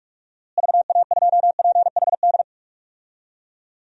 Morse code, heard as a keyed tone, about 46 wpm, 700 Hz. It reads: VA2PHD
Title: DXCC-100 Call Signs - 6 Characters - Repeated 2x - 45wpm